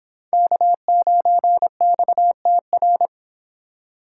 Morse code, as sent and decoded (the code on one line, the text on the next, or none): -.- ----. -..- - .-.
K9XTR